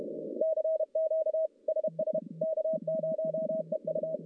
Morse code, heard as a keyed tone, about 31 wpm, 630 Hz. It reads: CQ SS K0EU
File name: contests / CW-SS-k5tr- -1200